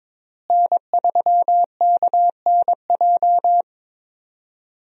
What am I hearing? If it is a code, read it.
Morse code, heard as a keyed tone, about 22 wpm, 700 Hz. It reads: N3KNJ
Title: USA Call Signs - 5 Characters - Repeated 2x - 22wpm